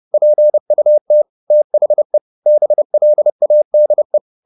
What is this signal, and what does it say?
Morse code, heard as a keyed tone, about 30 wpm, 600 Hz. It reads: PUT THE BLADE